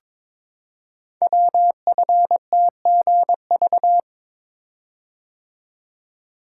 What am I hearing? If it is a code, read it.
Morse code, heard as a keyed tone, about 22 wpm, 700 Hz. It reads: WFTGV
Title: Code Groups - 5 Letters - 22wpm